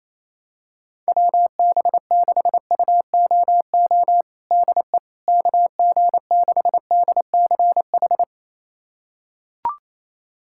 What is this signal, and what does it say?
Morse code, heard as a keyed tone, about 28 wpm, 700 Hz. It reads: WB6UOO DE KG6DCH
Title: USA Callsign DE Callsign 28wpm